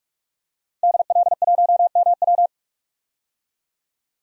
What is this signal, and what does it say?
Morse code, heard as a keyed tone, about 46 wpm, 700 Hz. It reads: DL1MW
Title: DXCC-100 Call Signs - 5 Characters - 45wpm